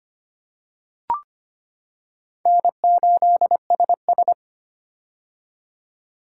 Morse code, heard as a keyed tone, about 25 wpm, 700 Hz. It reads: N8SS
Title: K1USN SST QSO Practice - 25wpm